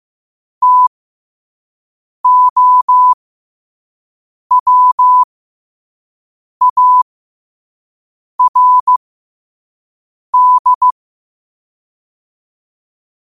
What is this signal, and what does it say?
Morse code, heard as a keyed tone, about 15 wpm, spaced as for 5 wpm, 1000 Hz. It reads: TOWARD
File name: qotdfiles / 5wpm